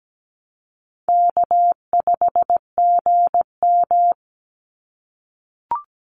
Morse code, heard as a keyed tone, about 17 wpm, 700 Hz. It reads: K5GM
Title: Call Signs - 17wpm